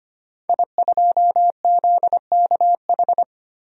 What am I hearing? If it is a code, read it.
Morse code, heard as a keyed tone, about 25 wpm, 700 Hz. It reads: I2ZKH
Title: DXCC-100 Call Signs - 5 Characters - 25wpm